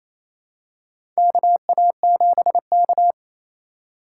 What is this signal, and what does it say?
Morse code, heard as a keyed tone, about 28 wpm, 700 Hz. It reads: KA7K
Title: USA Call Signs - 3x Character Spacing - 28wpm